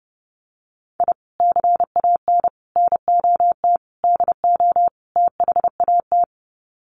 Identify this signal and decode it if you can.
Morse code, heard as a keyed tone, about 30 wpm, 700 Hz. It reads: I CAN NOT DO THAT
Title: Sets of 5 Words Encore2 30wpm